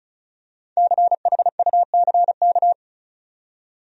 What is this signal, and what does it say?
Morse code, heard as a keyed tone, about 34 wpm, 700 Hz. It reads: CHUCK
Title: USA Common Names - Max 5 Letters - 35wpm